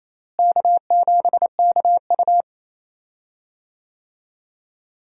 Morse code, heard as a keyed tone, about 28 wpm, 700 Hz. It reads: K7KU
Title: USA Call Signs - 4 Characters - Repeated 2x - 28wpm